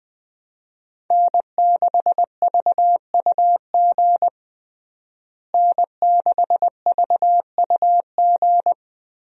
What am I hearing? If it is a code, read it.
Morse code, heard as a keyed tone, about 21 wpm, 700 Hz. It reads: N6VUG N6VUG
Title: USA Call Signs - 5 Characters - Repeated 2x - 20wpm